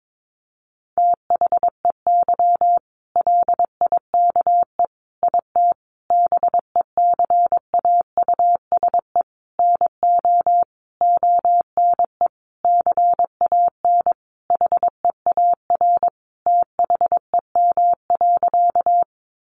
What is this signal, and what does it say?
Morse code, heard as a keyed tone, about 22 wpm, 700 Hz. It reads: THEY LIKE IT BECAUSE NO ONE CAN HEAR THEM.
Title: Sentences from Top 600 Words 22wpm